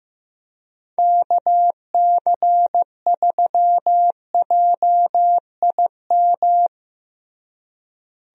Morse code, heard as a keyed tone, about 15 wpm, 700 Hz. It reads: KC3JIM
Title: USA Call Signs - 6 Characters - Repeated 3x - 15wpm